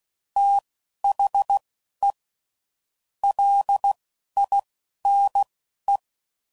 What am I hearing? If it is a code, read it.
Morse code, heard as a keyed tone, about 16 wpm, spaced as for 12 wpm, 800 Hz. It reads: THE LINE
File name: CW Code Groups @10WPM / Technical Subjects / SwPwrSup